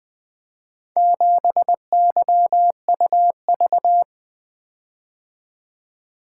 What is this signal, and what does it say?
Morse code, heard as a keyed tone, about 24 wpm, 700 Hz. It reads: 7YUV
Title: Speed-Racing - Four Letter-Number Combinations - 20wpm